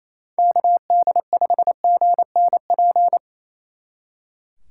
Morse code, heard as a keyed tone, about 28 wpm, 700 Hz. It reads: KD5GNP
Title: Speed Racing - USA 2x3 Call Signs 28wpm